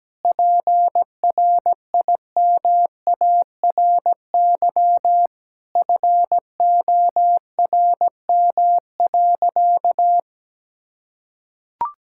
Morse code, heard as a keyed tone, about 17 wpm, 700 Hz. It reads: PRIMARY FORM.
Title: Sentences from Top 1000 Words - Review Part 5 - 17wpm